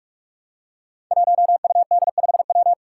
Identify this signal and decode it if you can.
Morse code, heard as a keyed tone, about 44 wpm, 700 Hz. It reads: 1UD5W
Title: Five Challenging Letter-Number Combinations 45wpm